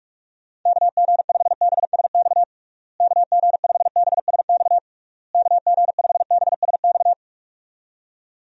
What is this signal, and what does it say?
Morse code, heard as a keyed tone, about 46 wpm, 700 Hz. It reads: KG5BSX KG5BSX KG5BSX
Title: USA Call Signs - 6 Characters - Repeated 3x - 45wpm